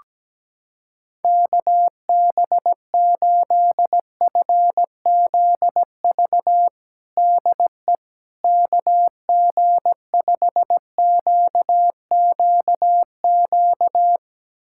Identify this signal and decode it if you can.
Morse code, heard as a keyed tone, about 17 wpm, 700 Hz. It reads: KB8FZV DE KG5QQQ
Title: USA Callsign DE Callsign - 6 Characters - 17wpm